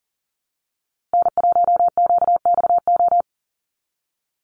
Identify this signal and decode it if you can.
Morse code, heard as a keyed tone, about 40 wpm, 700 Hz. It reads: N1QXO
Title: USA Call Signs Repeated 2x 40wpm